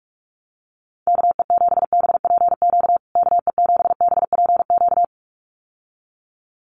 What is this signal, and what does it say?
Morse code, heard as a keyed tone, about 44 wpm, 700 Hz. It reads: KE7BPQ KE7BPQ
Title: USA Call Signs Repeated 2x 45wpm